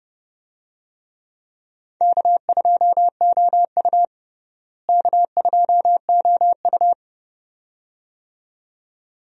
Morse code, heard as a keyed tone, about 30 wpm, 700 Hz. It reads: K2OU K2OU
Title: USA Call Signs - 4 Characters - Repeated 2x - 30wpm